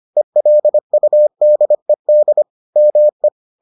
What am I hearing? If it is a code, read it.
Morse code, heard as a keyed tone, about 25 wpm, 600 Hz. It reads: ELUDED ME